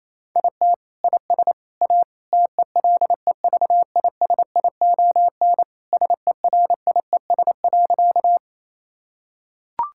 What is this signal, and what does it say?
Morse code, heard as a keyed tone, about 28 wpm, 700 Hz. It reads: IT IS A TELEVISION SERIES.
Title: Sentences from Top 1000 Words - Review Part 5 - 28wpm